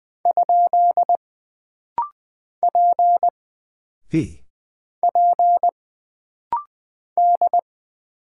Morse code, heard as a keyed tone, about 20 wpm, 700 Hz. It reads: ? P P D
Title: Single Character (Letter:Number:Punctuation:Prosign) - Rapid-Fire - 20wpm